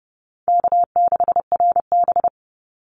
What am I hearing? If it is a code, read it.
Morse code, heard as a keyed tone, about 30 wpm, 700 Hz. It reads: K6RB
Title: Easy Call Signs - 30wpm